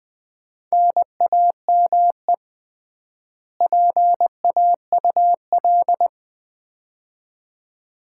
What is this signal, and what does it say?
Morse code, heard as a keyed tone, about 20 wpm, 700 Hz. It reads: NAME PAUL
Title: Common QSO Phrases - 3x Word-Spacing - 20wpm